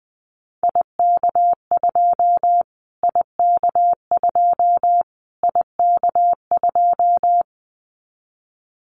Morse code, heard as a keyed tone, about 20 wpm, 700 Hz. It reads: IK2 IK2 IK2
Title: Callsign Prefixes for Top 25 DXCC - Repeated 3x - 20wpm